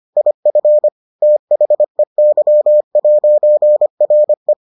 Morse code, heard as a keyed tone, about 25 wpm, 600 Hz. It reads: IF THEY'RE